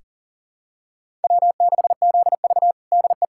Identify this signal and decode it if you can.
Morse code, heard as a keyed tone, about 40 wpm, 700 Hz. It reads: W6ZV DE
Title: USA Callsign DE Callsign - 4 Characters - 40wpm